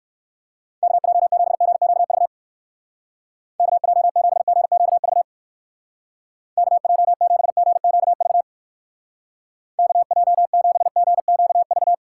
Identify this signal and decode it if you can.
Morse code, heard as a keyed tone, about 57 wpm, 700 Hz. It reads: KJ7GQV KJ7GQV KJ7GQV KJ7GQV